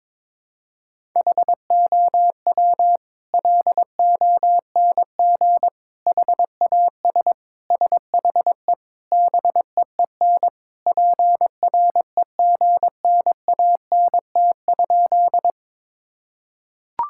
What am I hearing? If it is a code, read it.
Morse code, heard as a keyed tone, about 22 wpm, 700 Hz. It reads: HOW LONG HAS SHE BEEN PREGNANT?
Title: Sentences from Top 1000 Words - Review Part 6 - 22wpm